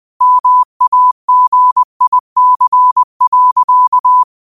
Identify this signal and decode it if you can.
Morse code, heard as a keyed tone, about 20 wpm, 1000 Hz. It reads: MAGIC.